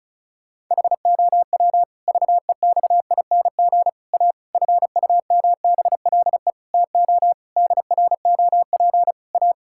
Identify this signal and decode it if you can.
Morse code, heard as a keyed tone, about 34 wpm, 700 Hz. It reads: HOW VEXING A FUMBLE TO DROP A
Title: Pangram Challenge v2 - 35wpm